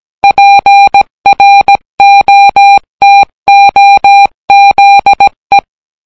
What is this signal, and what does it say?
Morse code, heard as a keyed tone, about 18 wpm, 800 Hz. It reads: PROTOZE